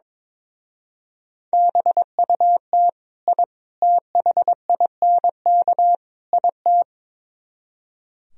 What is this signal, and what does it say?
Morse code, heard as a keyed tone, about 22 wpm, 700 Hz. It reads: BUT I THINK IT
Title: Speed-Racing - Sets of 4 Words 22wpm